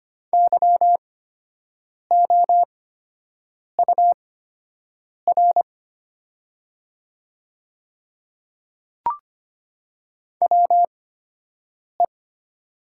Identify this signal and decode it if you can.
Morse code, heard as a keyed tone, about 25 wpm, spaced as for 7 wpm, 700 Hz. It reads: YOUR WE
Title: 8x Character Spacing - Top 100 Words - 25wpm